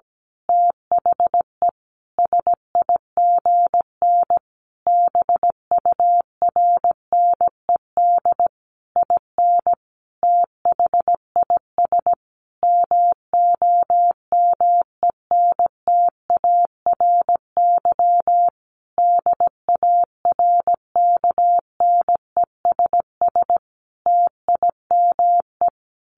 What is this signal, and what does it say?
Morse code, heard as a keyed tone, about 17 wpm, 700 Hz. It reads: THE SIGN BURNED IN THIS MOMENTARY DARKNESS TIME
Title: A Sound Of Thunder 17wpm